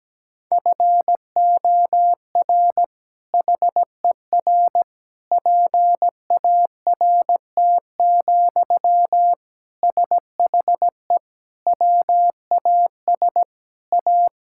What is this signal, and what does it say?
Morse code, heard as a keyed tone, about 17 wpm, 700 Hz. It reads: FOR HER PART, SHE WAS A
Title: Sentences from Top 1000 Words - Review Part 5 - 17wpm